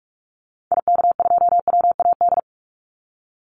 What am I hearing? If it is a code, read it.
Morse code, heard as a keyed tone, about 48 wpm, 700 Hz. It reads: IK2WAD
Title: Call Signs Repeated 2x - 45wpm